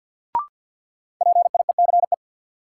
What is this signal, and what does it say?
Morse code, heard as a keyed tone, about 53 wpm, 700 Hz. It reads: PIECE
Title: Sets of 3 Words - New Word Drill - 50wpm